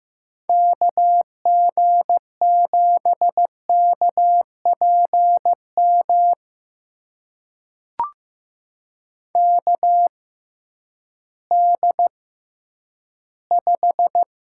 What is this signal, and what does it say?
Morse code, heard as a keyed tone, about 15 wpm, spaced as for 10 wpm, 700 Hz. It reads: KG7KPM KD5
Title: USA Call Signs - 6 Characters - 6x Character Spacing - 15wpm